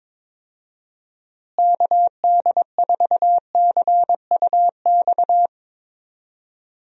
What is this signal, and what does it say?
Morse code, heard as a keyed tone, about 22 wpm, 700 Hz. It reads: KD4CUX
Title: USA Call Signs - 6 Characters - Repeated 2x - 22wpm